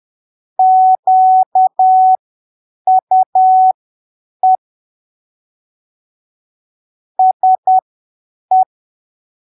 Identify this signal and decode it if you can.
Morse code, heard as a keyed tone, about 10 wpm, spaced as for 7 wpm, 750 Hz. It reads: QUE SE
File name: arquivos / texto / texto 08 10ppm